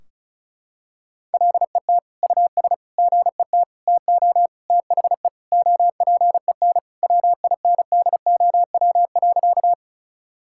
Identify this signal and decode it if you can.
Morse code, heard as a keyed tone, about 36 wpm, 700 Hz. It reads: LET US GET TO THE OPEN WINDOW.
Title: Sentences from Top 1600 Words 35wpm